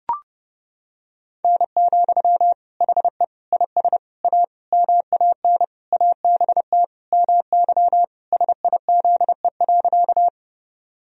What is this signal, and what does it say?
Morse code, heard as a keyed tone, about 32 wpm, 700 Hz. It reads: N, HE IS A MAN ABT MY SIZE.
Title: Sentences from Top 400 Words and Abbreviations 30wpm